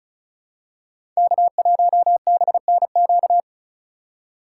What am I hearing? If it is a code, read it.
Morse code, heard as a keyed tone, about 36 wpm, 700 Hz. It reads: K1BNQ
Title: USA Call Signs - 5 Characters - 3x Character Spacing - 35wpm